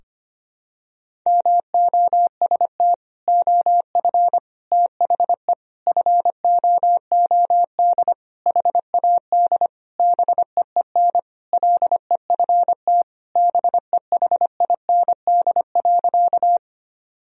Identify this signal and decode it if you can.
Morse code, heard as a keyed tone, about 25 wpm, 700 Hz. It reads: MOST OF THE FOOD HAD BEEN LEFT BEHIND.
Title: Sentences from Top 1000 Words - Review Part 1 - 25wpm